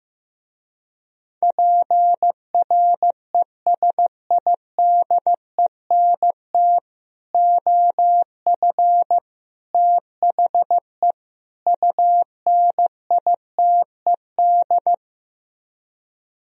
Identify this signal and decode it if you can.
Morse code, heard as a keyed tone, about 15 wpm, 700 Hz. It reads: PRESIDENT OF THE UNITED
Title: Speed-Racing - Sets of 4 Words 15wpm